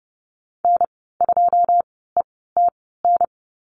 Morse code, heard as a keyed tone, about 30 wpm, spaced as for 17 wpm, 700 Hz. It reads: N2ETN
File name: Lesson 19g - Letter w - Call Signs 30wpm